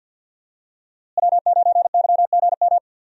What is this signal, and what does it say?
Morse code, heard as a keyed tone, about 51 wpm, 700 Hz. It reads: W9YGM
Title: USA Call Signs - 5 Characters - Repeated 3x - 50wpm